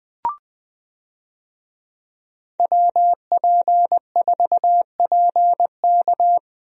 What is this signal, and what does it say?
Morse code, heard as a keyed tone, about 20 wpm, 700 Hz. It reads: WP4PK
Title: USA Call Signs - 5 Characters - Repeated 2x - 20wpm